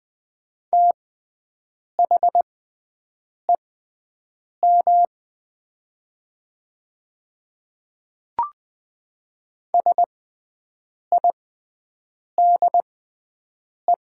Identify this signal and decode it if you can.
Morse code, heard as a keyed tone, about 20 wpm, spaced as for 7 wpm, 700 Hz. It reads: THEM SIDE
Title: 6x Character Spacing - Top 300 Words - 20wpm